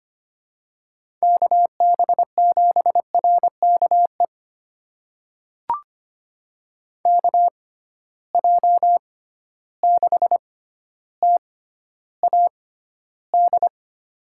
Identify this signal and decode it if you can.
Morse code, heard as a keyed tone, about 25 wpm, spaced as for 13 wpm, 700 Hz. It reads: KB7RKE KJ6TAD
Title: USA Call Signs - 6 Characters - 6x Character Spacing - 25wpm